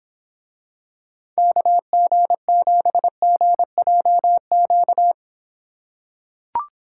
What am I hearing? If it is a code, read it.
Morse code, heard as a keyed tone, about 26 wpm, 700 Hz. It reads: KG7GJQ